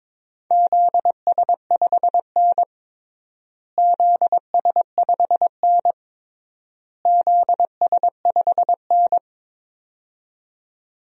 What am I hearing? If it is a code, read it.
Morse code, heard as a keyed tone, about 22 wpm, 700 Hz. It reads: ZS5N ZS5N ZS5N
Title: DXCC-100 Call Signs - 4 Characters - Repeated 3x - 22wpm